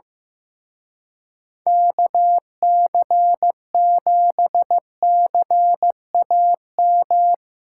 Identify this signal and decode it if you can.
Morse code, heard as a keyed tone, about 17 wpm, 700 Hz. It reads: KC7CAM
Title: Speed Racing - USA 2x3 Call Signs 15wpm